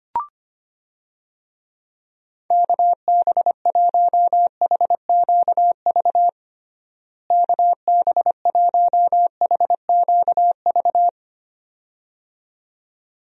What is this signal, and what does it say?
Morse code, heard as a keyed tone, about 25 wpm, 700 Hz. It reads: KB1HQV KB1HQV
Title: USA Call Signs - 6 Characters - Repeated 2x - 25wpm